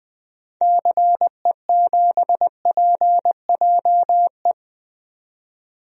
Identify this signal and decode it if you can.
Morse code, heard as a keyed tone, about 20 wpm, 700 Hz. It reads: CE7PJE
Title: DXCC-100 Call Signs - 6 Characters - 20wpm